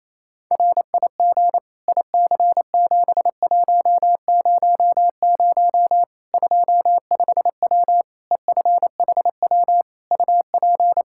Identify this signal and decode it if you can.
Morse code, heard as a keyed tone, about 28 wpm, 700 Hz. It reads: RIG IC7100 25W EFHW UP
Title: Practice QSO - Exchange 2 - Info - 28wpm